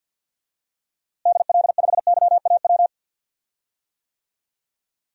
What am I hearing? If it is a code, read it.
Morse code, heard as a keyed tone, about 51 wpm, 700 Hz. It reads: DL5YAW
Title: DXCC-100 Call Signs - 6 Characters - Repeated 3x - 50wpm